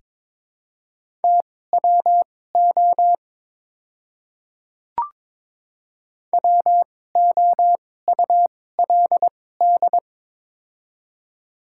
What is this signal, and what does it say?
Morse code, heard as a keyed tone, about 22 wpm, spaced as for 16 wpm, 700 Hz. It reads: TWO WOULD
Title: 2x Character Spacing - Top 100 Words - 22wpm